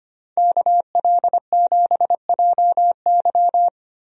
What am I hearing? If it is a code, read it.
Morse code, heard as a keyed tone, about 25 wpm, 700 Hz. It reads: KL7JY
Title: DXCC-100 Call Signs - 5 Characters - Repeated 2x - 25wpm